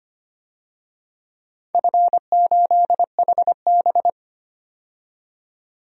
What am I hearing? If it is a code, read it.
Morse code, heard as a keyed tone, about 25 wpm, 700 Hz. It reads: F8HB
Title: DXCC-100 Call Signs - 4 Characters - 25wpm